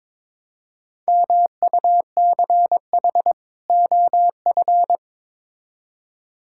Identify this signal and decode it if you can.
Morse code, heard as a keyed tone, about 21 wpm, 700 Hz. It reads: MUCH OF
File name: Build A Phrase - 2-Word Phrases - 22wpm